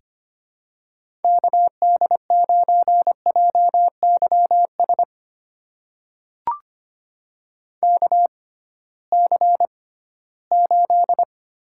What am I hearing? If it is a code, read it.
Morse code, heard as a keyed tone, about 26 wpm, spaced as for 16 wpm, 700 Hz. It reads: KD9JYS KC8
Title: USA Call Signs - 6 Characters - 6x Character Spacing - 25wpm